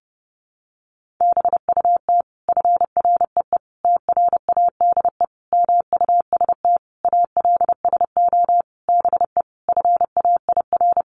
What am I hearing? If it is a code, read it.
Morse code, heard as a keyed tone, about 30 wpm, 700 Hz. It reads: BUT FREE TRADE MUST ALSO BE FAIR
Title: Sentences from Top 1300 Words 30wpm